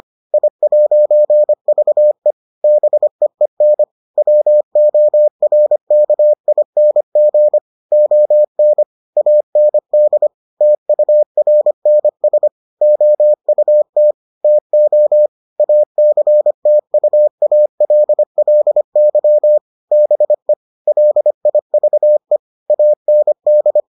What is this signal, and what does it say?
Morse code, heard as a keyed tone, about 25 wpm, 600 Hz. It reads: I'VE BEEN WORKING ON AND TURNS OUT TO ACTUALLY BE LIVE AND